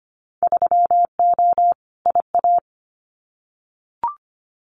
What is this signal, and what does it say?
Morse code, heard as a keyed tone, about 25 wpm, 700 Hz. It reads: 3O IA
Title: Winter Field Day Exchanges 25wpm